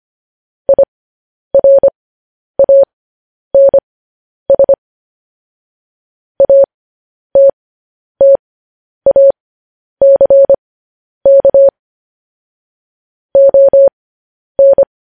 Morse code, heard as a keyed tone, about 25 wpm, spaced as for 10 wpm, 550 Hz. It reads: IRANS ATTACK ON
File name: podcast / News headlines-1x-25x10